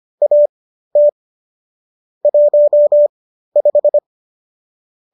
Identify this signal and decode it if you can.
Morse code, heard as a keyed tone, about 25 wpm, spaced as for 13 wpm, 590 Hz. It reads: AT 15